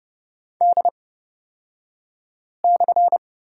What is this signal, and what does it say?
Morse code, heard as a keyed tone, about 30 wpm, spaced as for 5 wpm, 700 Hz. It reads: D/
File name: Lesson 37d - Letter x - Two Characters 30wpm